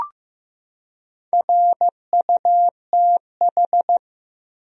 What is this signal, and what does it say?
Morse code, heard as a keyed tone, about 15 wpm, 700 Hz. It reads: RUTH
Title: USA Common Names - Max 4 Letters - 15wpm